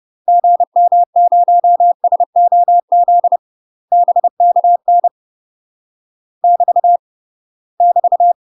Morse code, heard as a keyed tone, about 30 wpm, 700 Hz. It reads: GM0SOZ BKN <BT> <BT>
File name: UK MASTER QSOs TRAINING / 30WPM / UK MASTER QSOs 0002